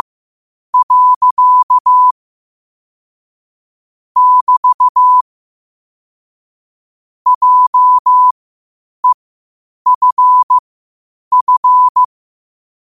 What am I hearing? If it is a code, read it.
Morse code, heard as a keyed tone, about 15 wpm, spaced as for 8 wpm, 1000 Hz. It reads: . <BT> JEFF